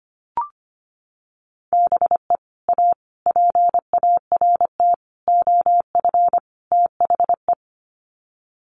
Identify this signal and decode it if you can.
Morse code, heard as a keyed tone, about 25 wpm, 700 Hz. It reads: BE A PART OF THE
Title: Sets of 5 Words Encore 25wpm